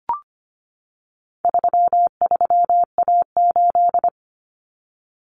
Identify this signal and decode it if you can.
Morse code, heard as a keyed tone, about 25 wpm, 700 Hz. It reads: 33A8
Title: Common Cut Numbers - 4-Digit 25wpm